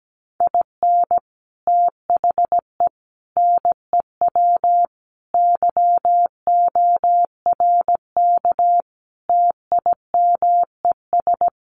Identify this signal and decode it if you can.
Morse code, heard as a keyed tone, about 17 wpm, 700 Hz. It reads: IN THE NEW YORK TIMES
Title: Sets of 5 Words 17wpm